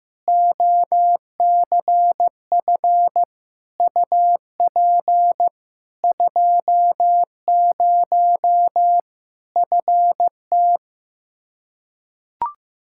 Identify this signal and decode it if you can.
Morse code, heard as a keyed tone, about 15 wpm, 700 Hz. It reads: OCF UP 20 FT
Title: Practice QSO - Exchange 2 - Info - 15wpm